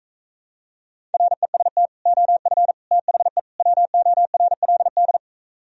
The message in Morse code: REST OF THE WORLD

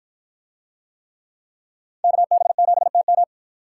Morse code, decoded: KB7TM

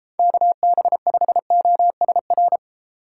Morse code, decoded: KB5OSR

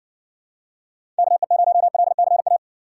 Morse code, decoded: KE0LCA

